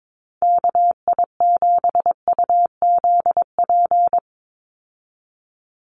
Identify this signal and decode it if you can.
Morse code, heard as a keyed tone, about 22 wpm, 700 Hz. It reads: KI7UZP
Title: USA Call Signs 22wpm